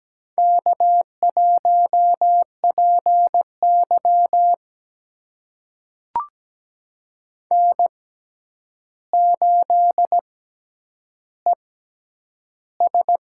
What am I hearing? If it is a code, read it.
Morse code, heard as a keyed tone, about 17 wpm, spaced as for 9 wpm, 700 Hz. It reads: K1PY N8ES
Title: USA Call Signs - 4 Characters - 6x Character Spacing - 17wpm